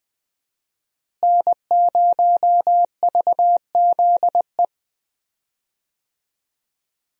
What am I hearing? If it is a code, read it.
Morse code, heard as a keyed tone, about 20 wpm, 700 Hz. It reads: N0VZE
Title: USA Call Signs - 5 Characters - Repeated 3x - 20wpm